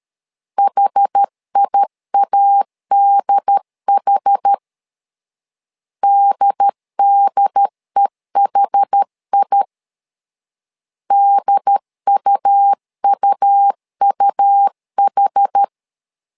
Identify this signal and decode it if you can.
Morse code, heard as a keyed tone, about 13 wpm, 800 Hz. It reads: HIADH DDEHI DUUUH
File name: LEZIONI / 2 UHDN / 2 UHD 12wpm 1540